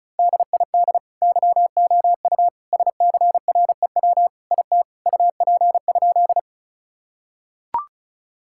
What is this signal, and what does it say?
Morse code, heard as a keyed tone, about 36 wpm, 700 Hz. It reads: DID YOU SCREW IT UP?
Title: Sentences from Top 1600 Words 35wpm